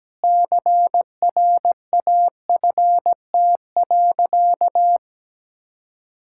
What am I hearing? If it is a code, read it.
Morse code, heard as a keyed tone, about 17 wpm, 700 Hz. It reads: CRAFT.
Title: Sentences from Top 1000 Words - Review Part 6 - 17wpm